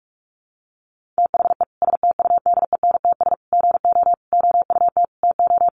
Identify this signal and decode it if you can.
Morse code, heard as a keyed tone, about 44 wpm, 700 Hz. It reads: THE STUDENTS GO OUT TO
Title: Easy Phrases 45wpm